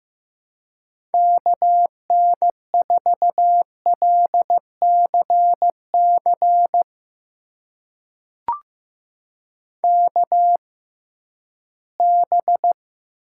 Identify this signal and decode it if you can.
Morse code, heard as a keyed tone, about 15 wpm, spaced as for 11 wpm, 700 Hz. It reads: KN4LCC KB
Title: USA Call Signs - 6 Characters - 6x Character Spacing - 15wpm